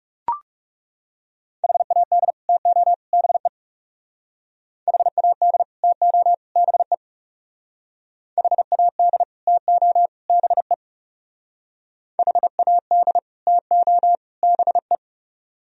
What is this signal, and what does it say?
Morse code, heard as a keyed tone, about 36 wpm, 700 Hz. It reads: HAD TO BE HAD TO BE HAD TO BE HAD TO BE
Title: Speed-Racing - Sets of 3 Words 30wpm